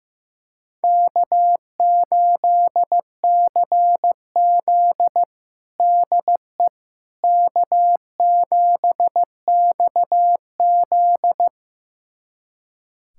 Morse code, heard as a keyed tone, about 15 wpm, 700 Hz. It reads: K8CZ DE K7XZ
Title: USA Callsign DE Callsign - 4 Characters - 15wpm